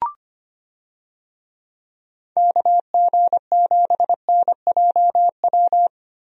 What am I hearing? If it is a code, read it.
Morse code, heard as a keyed tone, about 25 wpm, 700 Hz. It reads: KG7NJW
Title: USA Call Signs - 6 Characters - 25wpm